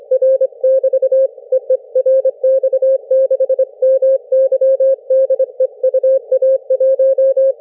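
Morse code, heard as a keyed tone, about 25 wpm, 530 Hz. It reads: R <BT> EE RX6MY DE UA1
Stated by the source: CW QSO between UA1CEG and RX6MY on homebrew rigs, three watts QRP into a delta antenna in forest tree branches, heard thousands of kilometres away; RU-QRP Club.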